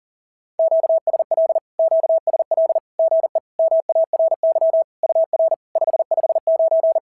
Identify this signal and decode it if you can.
Morse code, heard as a keyed tone, about 40 wpm, 690 Hz. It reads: QSL QSL GE MARY UR 559
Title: SOTA QSO Practice 40wpm